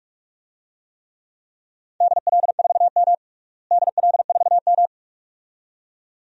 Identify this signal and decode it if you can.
Morse code, heard as a keyed tone, about 46 wpm, 700 Hz. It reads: DL4M DL4M
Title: DXCC-100 Call Signs - 4 Characters - Repeated 2x - 45wpm